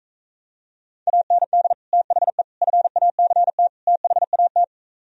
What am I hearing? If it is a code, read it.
Morse code, heard as a keyed tone, about 45 wpm, 700 Hz. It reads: AND THE FACT THAT